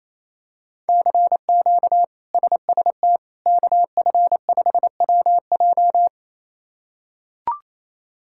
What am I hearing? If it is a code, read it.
Morse code, heard as a keyed tone, about 28 wpm, 700 Hz. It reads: CQ SST KF5WJ
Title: K1USN SST QSO Practice - 28wpm